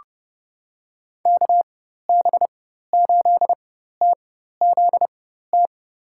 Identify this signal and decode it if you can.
Morse code, heard as a keyed tone, about 32 wpm, spaced as for 14 wpm, 700 Hz. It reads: KB8TZT
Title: USA Call Signs - 4x Character Spacing - 30wpm